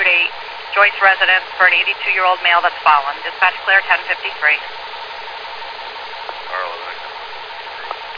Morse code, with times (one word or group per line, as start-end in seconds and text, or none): none